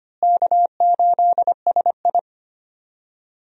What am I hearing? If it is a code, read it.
Morse code, heard as a keyed tone, about 25 wpm, 700 Hz. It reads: K8SI
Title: USA Call Signs - 4 Characters - Repeated 3x - 25wpm